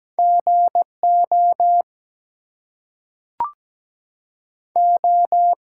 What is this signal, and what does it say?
Morse code, heard as a keyed tone, about 17 wpm, 700 Hz. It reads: GO O
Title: Two Letter Words - 17wpm